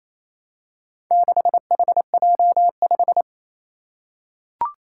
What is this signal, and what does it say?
Morse code, heard as a keyed tone, about 29 wpm, 700 Hz. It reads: 6HJ5
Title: Four Challenging Letter-Number Combinations 28wpm